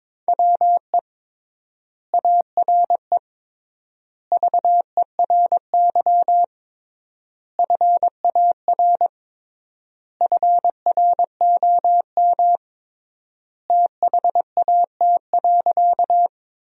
Morse code, heard as a keyed tone, about 22 wpm, 700 Hz. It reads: WE ARE VERY FAR FROM THAT.
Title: 3x Word-Spacing - Sentences from Top 300 Words - 22wpm